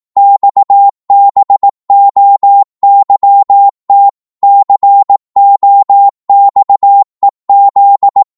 Morse code, heard as a keyed tone, about 18 wpm, 800 Hz. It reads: XBOYT COXEZ